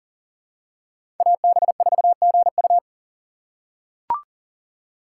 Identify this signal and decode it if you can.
Morse code, heard as a keyed tone, about 38 wpm, 700 Hz. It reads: AB4GU
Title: USA Call Signs - 4x Character Spacing - 40wpm